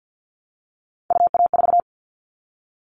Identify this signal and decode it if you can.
Morse code, heard as a keyed tone, about 50 wpm, 700 Hz. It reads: UA4
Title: Callsign Prefixes for Top 50 DXCC - 50wpm